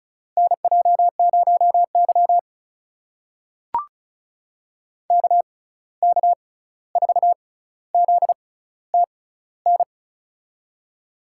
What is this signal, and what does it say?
Morse code, heard as a keyed tone, about 34 wpm, spaced as for 16 wpm, 700 Hz. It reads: NJ0Y KK4ZTN
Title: USA Call Signs - 6x Character Spacing - 35wpm